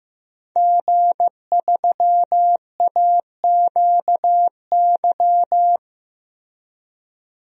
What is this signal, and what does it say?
Morse code, heard as a keyed tone, about 15 wpm, 700 Hz. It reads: G3AQY
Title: DXCC-100 Call Signs - 5 Characters - 15wpm